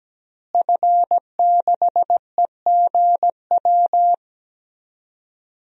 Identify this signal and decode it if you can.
Morse code, heard as a keyed tone, about 17 wpm, 700 Hz. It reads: F6EGW
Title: DXCC-100 Call Signs - 5 Characters - Repeated 3x - 17wpm